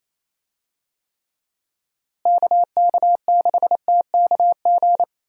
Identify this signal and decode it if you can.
Morse code, heard as a keyed tone, about 28 wpm, 700 Hz. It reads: KK6TKG
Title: USA Call Signs - 6 Characters - Repeated 3x - 28wpm